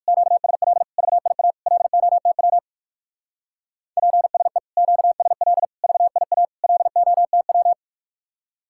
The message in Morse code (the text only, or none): QSL VIA LOTW PSE QSL VIA LOTW